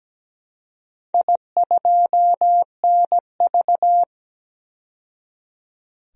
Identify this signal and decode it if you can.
Morse code, heard as a keyed tone, about 17 wpm, 700 Hz. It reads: I2NV
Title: DXCC-100 Call Signs - 4 Characters - 17wpm